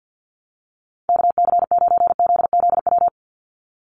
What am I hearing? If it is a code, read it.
Morse code, heard as a keyed tone, about 50 wpm, 700 Hz. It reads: KC9ZZW
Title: USA Call Signs 50wpm